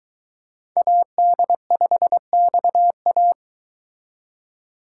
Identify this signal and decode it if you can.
Morse code, heard as a keyed tone, about 22 wpm, 700 Hz. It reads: AD5XA